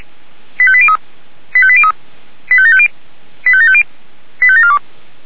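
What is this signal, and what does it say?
Five-tone selective call (ZVEI), heard as quick runs of five tones, tones of about 70 ms. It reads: 85803 85803 85750 85750 85742
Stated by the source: ZVEI five-tone call 85749 received on SDR.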